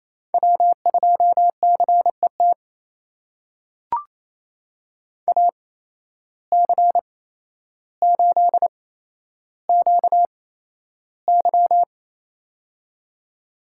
Morse code, heard as a keyed tone, about 28 wpm, spaced as for 12 wpm, 700 Hz. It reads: W2CET AC8QY
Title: USA Call Signs - 8x Character Spacing - 28wpm